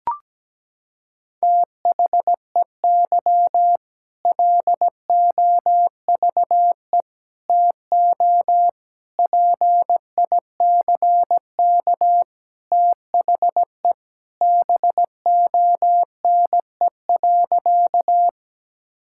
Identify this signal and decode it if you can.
Morse code, heard as a keyed tone, about 17 wpm, 700 Hz. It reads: THEY LOVE TO PICK THE BONE.
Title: Sentences from Top 1600 Words 17wpm